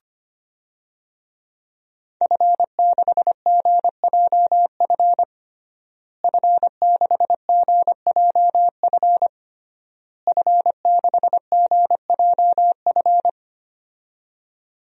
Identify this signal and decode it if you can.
Morse code, heard as a keyed tone, about 25 wpm, 700 Hz. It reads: F6GJF F6GJF F6GJF
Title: DXCC-100 Call Signs - 5 Characters - Repeated 3x - 25wpm